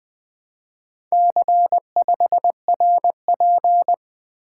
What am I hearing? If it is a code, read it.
Morse code, heard as a keyed tone, about 20 wpm, 700 Hz. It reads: C5RP
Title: Four Letter-Number Combinations - 20wpm